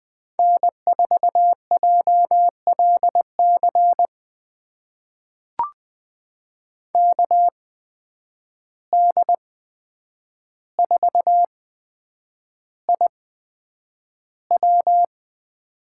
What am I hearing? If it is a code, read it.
Morse code, heard as a keyed tone, about 20 wpm, spaced as for 9 wpm, 700 Hz. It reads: N4JLC KD4IW
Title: USA Call Signs - 5 Characters - 8x Character Spacing - 20wpm